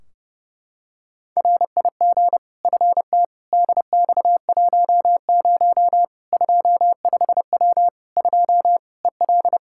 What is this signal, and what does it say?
Morse code, heard as a keyed tone, about 32 wpm, 700 Hz. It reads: RIG FT DX10 25W 2 EL
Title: Practice QSO - Exchange 2 - Info - 30wpm